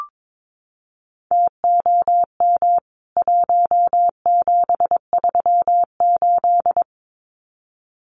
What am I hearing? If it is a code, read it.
Morse code, heard as a keyed tone, about 22 wpm, 700 Hz. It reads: TOM 1738
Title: CWT Exchange 22wpm